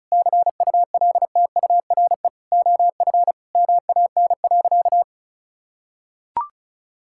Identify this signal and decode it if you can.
Morse code, heard as a keyed tone, about 36 wpm, 700 Hz. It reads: CULTURE OF MAN.
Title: Sentences from Top 1000 Words - Review Part 6 - 35wpm